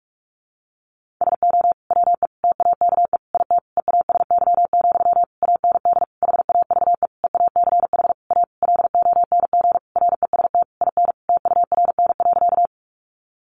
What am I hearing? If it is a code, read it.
Morse code, heard as a keyed tone, about 44 wpm, 700 Hz. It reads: SO WE TAKE IT EASY, AND HAVE EACH A LONG REST IN TURN.
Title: Sentences from Top 700 Words 45wpm